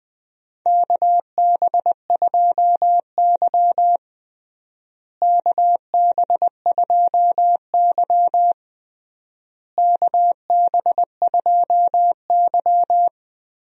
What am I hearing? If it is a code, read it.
Morse code, heard as a keyed tone, about 20 wpm, 700 Hz. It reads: KB2Y KB2Y KB2Y
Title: USA Call Signs - 4 Characters - Repeated 3x - 20wpm